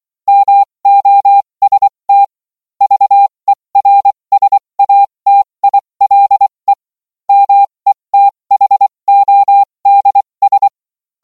Morse code, heard as a keyed tone, about 24 wpm, spaced as for 20 wpm, 800 Hz. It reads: MOST VERSATILE METHODS